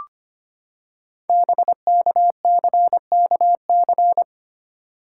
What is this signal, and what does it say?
Morse code, heard as a keyed tone, about 25 wpm, 690 Hz. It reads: BKCKC
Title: Code Groups - 5 Letters - 25wpm